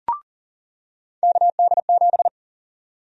Morse code, heard as a keyed tone, about 40 wpm, 700 Hz. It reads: KD7
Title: USA Call Sign Prefixes - 40wpm